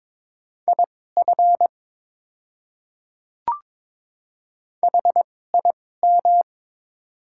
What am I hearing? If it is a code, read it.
Morse code, heard as a keyed tone, about 22 wpm, spaced as for 16 wpm, 700 Hz. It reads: IF HIM
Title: 2x Character Spacing - Top 100 Words - 22wpm